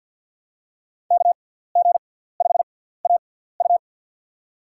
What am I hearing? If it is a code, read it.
Morse code, heard as a keyed tone, about 52 wpm, spaced as for 17 wpm, 700 Hz. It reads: KG5AU
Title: USA Call Signs - 5 Characters - 6x Character Spacing - 50wpm